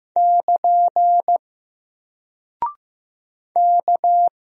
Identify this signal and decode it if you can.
Morse code, heard as a keyed tone, about 15 wpm, 700 Hz. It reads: <KN> K
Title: Single Character (Letter:Number:Punctuation:Prosign) Alt 15wpm